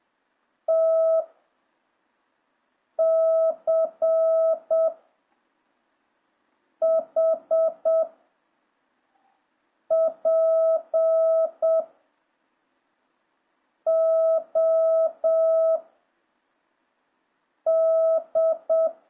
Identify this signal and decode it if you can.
Morse code, heard as a keyed tone, about 7 wpm, 650 Hz. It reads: T C H P O D